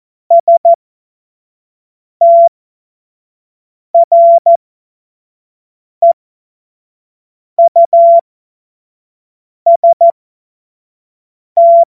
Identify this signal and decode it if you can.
Morse code, heard as a keyed tone, about 14 wpm, spaced as for 5 wpm, 680 Hz. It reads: STREUST
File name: Morsekurs / Witze / Witz 07. September 2021, 25 F70